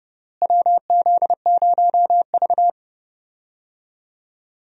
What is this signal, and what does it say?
Morse code, heard as a keyed tone, about 30 wpm, 700 Hz. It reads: WZ0V
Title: USA Call Signs - 4 Characters - Repeated 3x - 30wpm